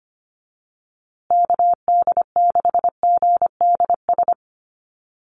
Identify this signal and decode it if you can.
Morse code, heard as a keyed tone, about 25 wpm, 700 Hz. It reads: KD6GDS
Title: USA Call Signs Repeated 3x 25wpm